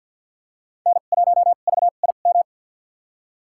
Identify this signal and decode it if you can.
Morse code, heard as a keyed tone, about 52 wpm, spaced as for 37 wpm, 700 Hz. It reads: N1VIM
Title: USA Call Signs - 5 Characters - 2x Character Spacing - 50wpm